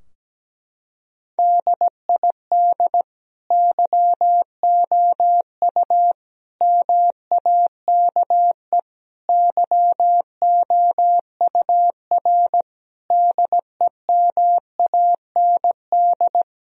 Morse code, heard as a keyed tone, about 17 wpm, 700 Hz. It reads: DID YOU MAKE YOUR DEMAND
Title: Sentences from Top 1000 Words - Review Part 4 - 17wpm